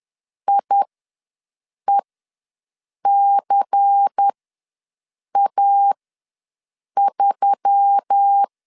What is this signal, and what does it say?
Morse code, heard as a keyed tone, about 10 wpm, spaced as for 6 wpm, 800 Hz. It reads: IECA3
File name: LEZIONI / 5 LBVC / 5 LBVC ws 10wpm 3733